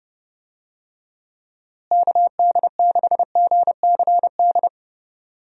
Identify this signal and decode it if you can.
Morse code, heard as a keyed tone, about 32 wpm, 700 Hz. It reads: KD6GCD
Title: USA Call Signs - 6 Characters - Repeated 2x - 30wpm